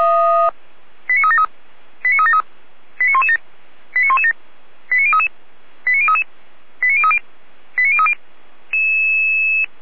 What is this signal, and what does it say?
Five-tone selective call (ZVEI), heard as quick runs of five tones, tones of about 70 ms. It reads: 89373 89373 89117 89117 89033 89033 89030 89030